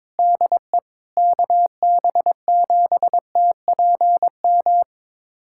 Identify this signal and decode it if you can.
Morse code, heard as a keyed tone, about 21 wpm, 700 Hz. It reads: DE KB7TPM
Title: USA Callsign DE Callsign - 6 Characters - 22wpm